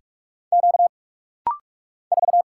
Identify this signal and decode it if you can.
Morse code, heard as a keyed tone, about 44 wpm, 700 Hz. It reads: Q 4
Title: Single Letter-Number - Rapid-Fire 45wpm